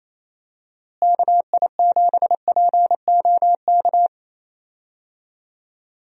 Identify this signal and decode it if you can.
Morse code, heard as a keyed tone, about 28 wpm, 700 Hz. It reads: KI7POK
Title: USA Call Signs - 6 Characters - Repeated 3x - 28wpm